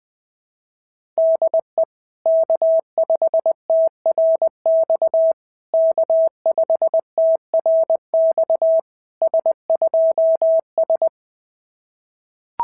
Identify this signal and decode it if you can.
Morse code, heard as a keyed tone, about 18 wpm, 650 Hz. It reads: DE K5TRX K5TRX S2S
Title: SOTA QSO Practice 20wpm